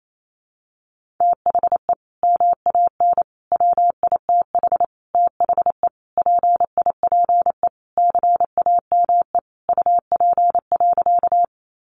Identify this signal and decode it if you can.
Morse code, heard as a keyed tone, about 28 wpm, 700 Hz. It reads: THE MAN WITH THE PIPE CAME UP.
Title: Sentences from Top 1500 Words 28wpm